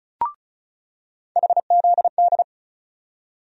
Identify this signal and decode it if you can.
Morse code, heard as a keyed tone, about 36 wpm, 700 Hz. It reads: HZD
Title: USA Call Sign Postfixes - 3 Letters 35wpm